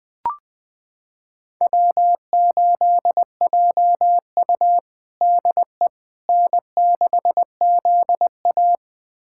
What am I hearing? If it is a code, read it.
Morse code, heard as a keyed tone, about 20 wpm, 700 Hz. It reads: W8JU DE N6ZA
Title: USA Callsign DE Callsign - 4 Characters - 20wpm